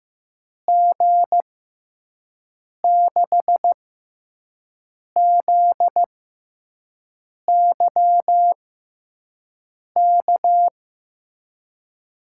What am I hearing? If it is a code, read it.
Morse code, heard as a keyed tone, about 15 wpm, spaced as for 5 wpm, 700 Hz. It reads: G6ZYK
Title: USA Call Signs - 6 Characters - 6x Character Spacing - 15wpm